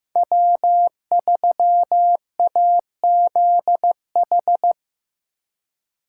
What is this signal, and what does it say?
Morse code, heard as a keyed tone, about 16 wpm, 700 Hz. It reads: W3AZH
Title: USA Call Signs - 5 Characters - 4x Character Spacing - 15wpm